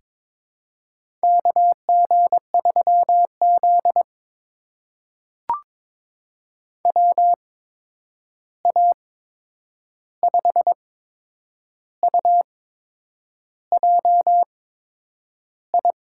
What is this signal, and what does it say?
Morse code, heard as a keyed tone, about 22 wpm, spaced as for 8 wpm, 700 Hz. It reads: KG3Z WA5UJI
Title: USA Call Signs - 8x Character Spacing - 22wpm